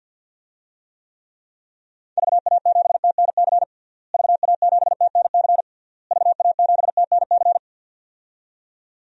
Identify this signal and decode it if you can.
Morse code, heard as a keyed tone, about 51 wpm, 700 Hz. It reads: VA7TNC VA7TNC VA7TNC
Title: DXCC-100 Call Signs - 6 Characters - Repeated 3x - 50wpm